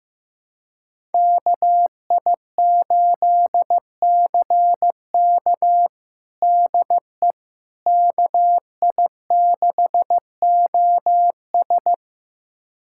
Morse code, heard as a keyed tone, about 15 wpm, 700 Hz. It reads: KI8CK DE KI6OS
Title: USA Callsign DE Callsign - 5 Characters - 15wpm